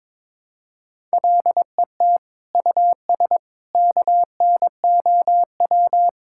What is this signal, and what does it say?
Morse code, heard as a keyed tone, about 22 wpm, 700 Hz. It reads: LET US KNOW
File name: practice / 22wpm / Sets of 6 Words - 22wpm